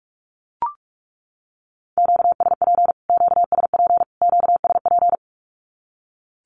Morse code, heard as a keyed tone, about 44 wpm, 700 Hz. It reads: QSP QSP QSP
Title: Q-Codes Repeated 3x 45wpm